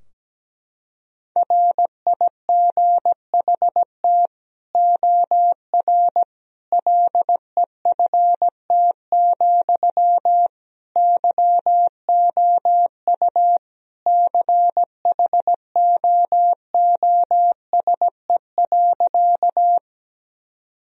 Morse code, heard as a keyed tone, about 17 wpm, 700 Hz. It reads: RIGHT OR LEFT, YOU CHOOSE.
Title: Sentences from Top 1000 Words - Review Part 4 - 17wpm